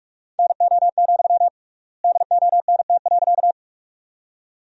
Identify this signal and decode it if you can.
Morse code, heard as a keyed tone, about 47 wpm, 700 Hz. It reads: NO, DONT.
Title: 3x Word-Spacing - Sentences from Top 300 Words - 45wpm